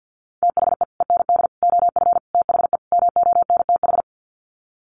Morse code, heard as a keyed tone, about 50 wpm, 700 Hz. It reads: THE END OF THE MONTH
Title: Sets of 5 Words Encore 50wpm